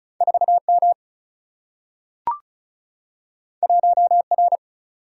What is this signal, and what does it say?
Morse code, heard as a keyed tone, about 36 wpm, 700 Hz. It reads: 4M 1R
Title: Single-Letter Single-Number 35wpm